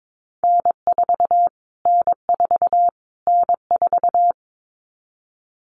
Morse code, heard as a keyed tone, about 22 wpm, 700 Hz. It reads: N4 N4 N4
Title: Callsign Prefixes for Top 25 DXCC - Repeated 3x - 22wpm